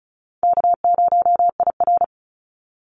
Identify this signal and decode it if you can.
Morse code, heard as a keyed tone, about 34 wpm, 700 Hz. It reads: K0IR
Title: Call Signs - 35wpm